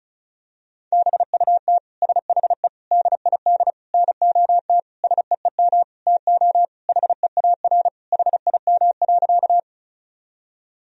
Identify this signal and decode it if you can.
Morse code, heard as a keyed tone, about 36 wpm, 700 Hz. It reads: BUT SHE DID NOT SEEM TO HEAR HIM.
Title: Sentences from Top 1000 Words - Review Part 1 - 35wpm